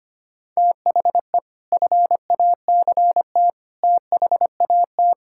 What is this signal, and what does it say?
Morse code, heard as a keyed tone, about 25 wpm, 700 Hz. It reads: THE FACT THAT
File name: practice / 25wpm / Sentences with 4-Word Phrases - 25wpm